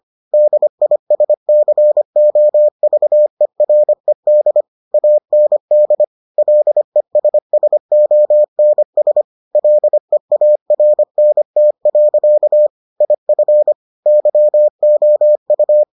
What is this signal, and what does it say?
Morse code, heard as a keyed tone, about 25 wpm, 600 Hz. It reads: DISCOVERED AND LESSONS LEARNT. IF YOU